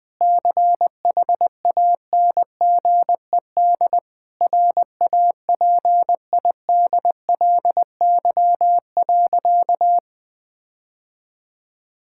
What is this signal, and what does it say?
Morse code, heard as a keyed tone, about 21 wpm, 700 Hz. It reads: CHANGED RAPIDLY.
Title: Sentences with 5-Word Phrases - Spoken Beginning - 20wpm